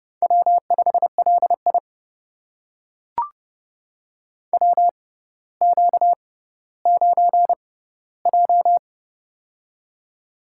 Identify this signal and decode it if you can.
Morse code, heard as a keyed tone, about 28 wpm, spaced as for 15 wpm, 700 Hz. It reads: W5LI WQ9J
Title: USA Call Signs - 6x Character Spacing - 30wpm